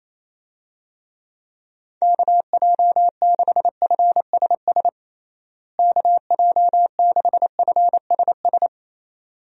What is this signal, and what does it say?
Morse code, heard as a keyed tone, about 28 wpm, 700 Hz. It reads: KJ6FSS KJ6FSS
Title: USA Call Signs - 6 Characters - Repeated 2x - 28wpm